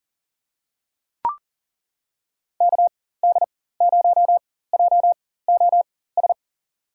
Morse code, heard as a keyed tone, about 40 wpm, spaced as for 19 wpm, 700 Hz. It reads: KD0JOS
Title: USA Call Signs - 6 Characters - 4x Character Spacing - 40wpm